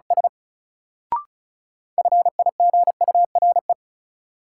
Morse code, heard as a keyed tone, about 36 wpm, 700 Hz. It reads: S FIGURE
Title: Sentences from Top 1000 Words - Review Part 2 - New Word Drill - 35wpm